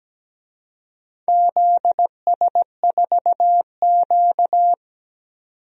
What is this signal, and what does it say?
Morse code, heard as a keyed tone, about 17 wpm, 700 Hz. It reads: ZS4Q
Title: DXCC-100 Call Signs - 4 Characters - 17wpm